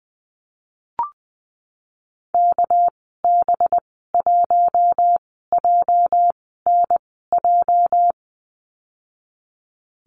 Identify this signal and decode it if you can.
Morse code, heard as a keyed tone, about 20 wpm, spaced as for 15 wpm, 700 Hz. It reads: KB1JNJ
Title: 207 Lesson 33g - Letter j - Call Signs 20wpm